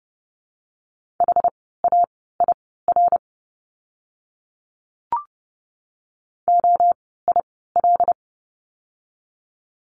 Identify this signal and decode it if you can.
Morse code, heard as a keyed tone, about 30 wpm, spaced as for 17 wpm, 700 Hz. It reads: HAIR OIL
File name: Lesson 17f - Letter u - Words 30wpm